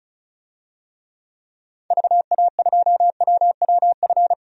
Morse code, heard as a keyed tone, about 36 wpm, 700 Hz. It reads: VA2WWF
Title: DXCC-100 Call Signs - 6 Characters - 35wpm